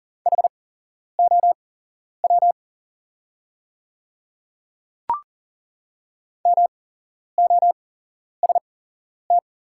Morse code, heard as a keyed tone, about 39 wpm, spaced as for 11 wpm, 690 Hz. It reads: HOW MOST
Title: 8x Character Spacing - Top 200 Words - 40wpm